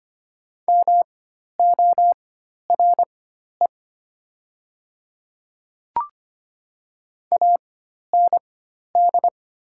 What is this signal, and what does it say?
Morse code, heard as a keyed tone, about 25 wpm, spaced as for 12 wpm, 700 Hz. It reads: MORE AND
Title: 4x Character Spacing - Top 200 Words - 25wpm